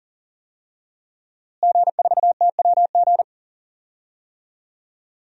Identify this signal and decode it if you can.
Morse code, heard as a keyed tone, about 40 wpm, 700 Hz. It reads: G4TWG
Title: DXCC-100 Call Signs - 5 Characters - Repeated 2x - 40wpm